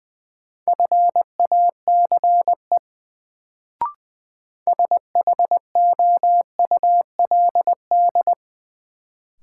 Morse code, heard as a keyed tone, about 21 wpm, 700 Hz. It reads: FACE SHOULD
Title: Sentences from Top 300 Words - New Word Drill - 20wpm